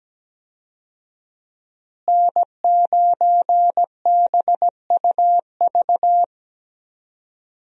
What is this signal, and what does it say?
Morse code, heard as a keyed tone, about 17 wpm, 700 Hz. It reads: N9BUV
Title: USA Call Signs - 5 Characters - Repeated 3x - 17wpm